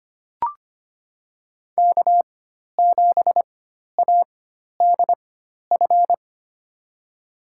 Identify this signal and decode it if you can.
Morse code, heard as a keyed tone, about 25 wpm, spaced as for 12 wpm, 700 Hz. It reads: K7ADF
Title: USA Call Signs - 5 Characters - 4x Character Spacing - 25wpm